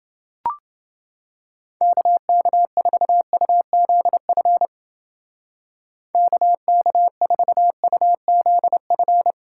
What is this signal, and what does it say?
Morse code, heard as a keyed tone, about 28 wpm, 700 Hz. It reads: KK4UZF KK4UZF